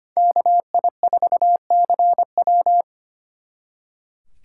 Morse code, heard as a keyed tone, about 32 wpm, 700 Hz. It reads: KI4CW